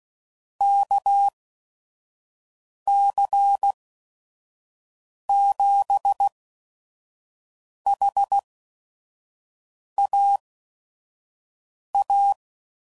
Morse code, heard as a keyed tone, about 16 wpm, spaced as for 5 wpm, 800 Hz. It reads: KC7HAA
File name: CW Code Groups @5WPM / QSOs / kc7haa